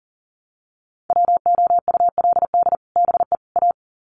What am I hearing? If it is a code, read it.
Morse code, heard as a keyed tone, about 40 wpm, 700 Hz. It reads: WOULD BE A
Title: Sets of 3 Words 40wpm